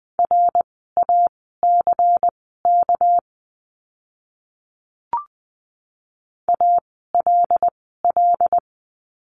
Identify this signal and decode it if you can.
Morse code, heard as a keyed tone, about 20 wpm, spaced as for 15 wpm, 700 Hz. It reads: RACK ALL
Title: 199 Lesson 32f - Letter k - Words 20wpm